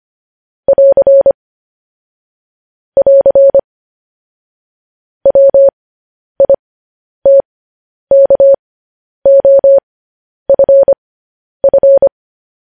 Morse code, heard as a keyed tone, about 25 wpm, spaced as for 10 wpm, 550 Hz. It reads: <AR> <AR> WITKOFF
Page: news headlines in morse code at 10 wpm